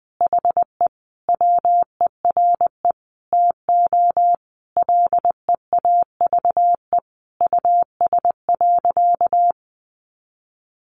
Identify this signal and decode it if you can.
Morse code, heard as a keyed tone, about 20 wpm, 700 Hz. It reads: HE WERE TO LEAVE US.
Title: Sentences from Top 400 Words 20wpm